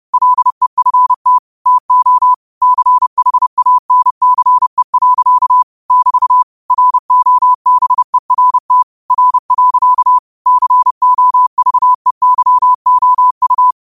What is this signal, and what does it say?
Morse code, heard as a keyed tone, about 30 wpm, 1000 Hz. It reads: LEFT TO CHANCE. <BT> ROBERT R. COVEYOU
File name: qotdfiles / 30wpm